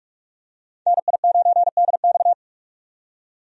Morse code, heard as a keyed tone, about 46 wpm, 700 Hz. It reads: NI9DX
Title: USA Call Signs - 5 Characters - 45wpm